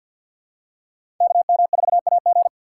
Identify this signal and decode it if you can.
Morse code, heard as a keyed tone, about 47 wpm, 700 Hz. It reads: KM4AG
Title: USA Call Signs - 3x Character Spacing - 50wpm